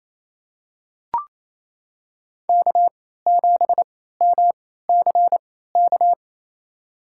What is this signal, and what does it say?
Morse code, heard as a keyed tone, about 28 wpm, spaced as for 16 wpm, 700 Hz. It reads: K7MCK
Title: USA Call Signs - 3x Character Spacing - 28wpm